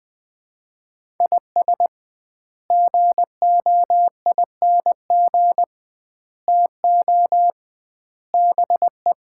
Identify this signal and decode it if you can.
Morse code, heard as a keyed tone, about 21 wpm, 700 Hz. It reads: IS GOING TO BE
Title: 2x Word-Spacing - Sets of 4 Words - 20wpm